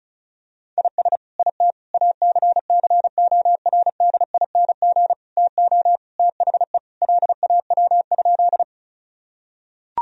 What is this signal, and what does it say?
Morse code, heard as a keyed tone, about 36 wpm, 700 Hz. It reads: IS IT ACCORDING TO THE LAW?
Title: Sentences from Top 1000 Words - Review Part 4 - 35wpm